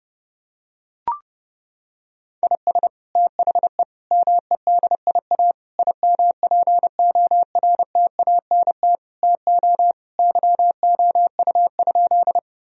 Morse code, heard as a keyed tone, about 30 wpm, 700 Hz. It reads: IS THE MEDIA IMPORTANT TO YOU?
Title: Sentences from Top 1000 Words - Review Part 4 - 30wpm